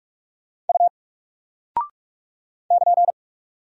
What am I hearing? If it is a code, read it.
Morse code, heard as a keyed tone, about 44 wpm, 700 Hz. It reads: U <KN>
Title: Single Character (Letter:Number:Punctuation:Prosign) Alt 45wpm